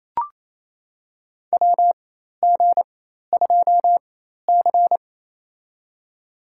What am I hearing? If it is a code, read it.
Morse code, heard as a keyed tone, about 27 wpm, spaced as for 13 wpm, 700 Hz. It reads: WG2C
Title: USA Call Signs - 4 Characters - 4x Character Spacing - 28wpm